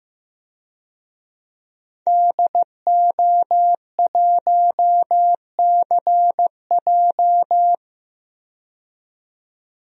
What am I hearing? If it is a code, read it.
Morse code, heard as a keyed tone, about 15 wpm, 700 Hz. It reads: DO1CJ
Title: DXCC-100 Call Signs - 5 Characters - 15wpm